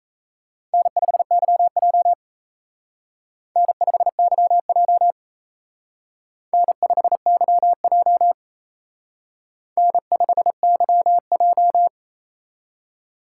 Morse code, heard as a keyed tone, about 35 wpm, 700 Hz. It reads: N5YJ N5YJ N5YJ N5YJ
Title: Speed Racing - USA 1x2 Call Signs 28wpm